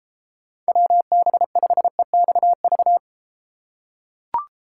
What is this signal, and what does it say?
Morse code, heard as a keyed tone, about 31 wpm, 700 Hz. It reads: WB5EXV